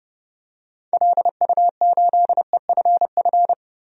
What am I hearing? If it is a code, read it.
Morse code, heard as a keyed tone, about 31 wpm, 700 Hz. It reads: LU8EFF
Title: DXCC-100 Call Signs - 6 Characters - 30wpm